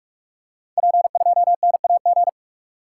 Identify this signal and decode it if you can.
Morse code, heard as a keyed tone, about 44 wpm, 700 Hz. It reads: P2NAG
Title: Code Groups - 5 Letter:Numbers - 45wpm